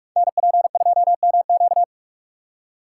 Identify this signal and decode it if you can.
Morse code, heard as a keyed tone, about 44 wpm, 700 Hz. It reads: NP2MQ
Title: Code Groups - 5 Letter:Numbers - 45wpm